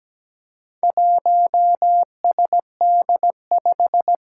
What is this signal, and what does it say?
Morse code, heard as a keyed tone, about 17 wpm, 700 Hz. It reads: 1SD5
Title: Four Challenging Letter-Number Combinations 17wpm